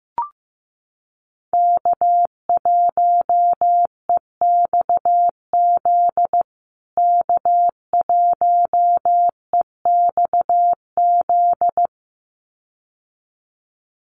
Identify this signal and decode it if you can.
Morse code, heard as a keyed tone, about 15 wpm, 700 Hz. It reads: K1EXZ K1EXZ
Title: USA Call Signs Repeated 2x 15wpm